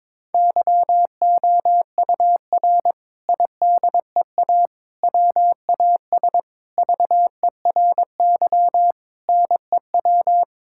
Morse code, heard as a keyed tone, about 21 wpm, 700 Hz. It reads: YOUR IDEA WAS VERY NEW
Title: Sentences from Top 1000 Words - Review Part 6 - 22wpm